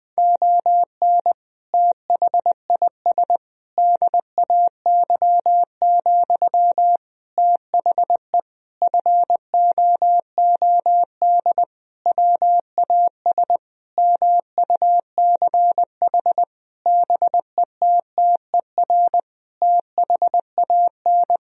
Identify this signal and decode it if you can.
Morse code, heard as a keyed tone, about 20 wpm, 700 Hz. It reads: ON THIS DAY, THE FOOD WAS MUCH BETTER THAN
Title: Sentences from Top 1000 Words - Review Part 2 - 20wpm